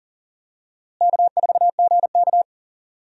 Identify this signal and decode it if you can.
Morse code, heard as a keyed tone, about 40 wpm, 700 Hz. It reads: K4GK
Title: USA Call Signs - 4 Characters - Repeated 3x - 40wpm